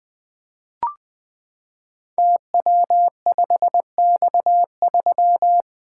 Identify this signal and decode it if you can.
Morse code, heard as a keyed tone, about 20 wpm, 700 Hz. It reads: TW5X3
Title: Code Groups - 5 Letter:Numbers - 20wpm